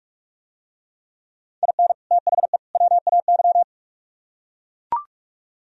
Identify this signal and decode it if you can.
Morse code, heard as a keyed tone, about 44 wpm, 700 Hz. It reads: IN THE WAY